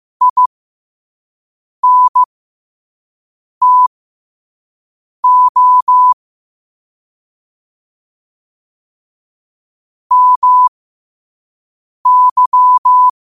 Morse code, heard as a keyed tone, about 15 wpm, spaced as for 5 wpm, 1000 Hz. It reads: INTO MY
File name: qotdfiles / 5wpm